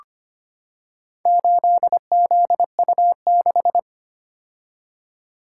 Morse code, heard as a keyed tone, about 25 wpm, 700 Hz. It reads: 8ZU6
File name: practice / 25wpm / Four Letter-Number Combinations - 25wpm